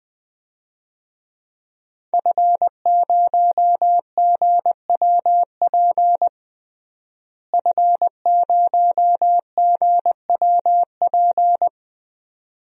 Morse code, heard as a keyed tone, about 20 wpm, 700 Hz. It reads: F0GWP F0GWP
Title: DXCC-100 Call Signs - 5 Characters - Repeated 2x - 20wpm